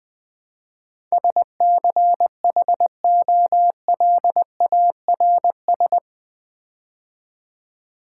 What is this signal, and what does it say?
Morse code, heard as a keyed tone, about 20 wpm, 700 Hz. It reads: SCHOLARS
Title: Build A Word - 4 Letters At A Time - 11-Letter Words - 20wpm